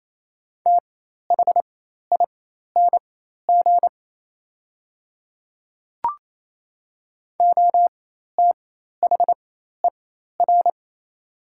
Morse code, heard as a keyed tone, about 28 wpm, spaced as for 13 wpm, 700 Hz. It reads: THING OTHER
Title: 4x Character Spacing - Top 200 Words - 28wpm